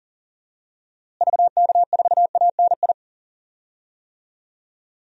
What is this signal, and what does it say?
Morse code, heard as a keyed tone, about 40 wpm, 700 Hz. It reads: VK4ANI
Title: DXCC-100 Call Signs - 6 Characters - Repeated 2x - 40wpm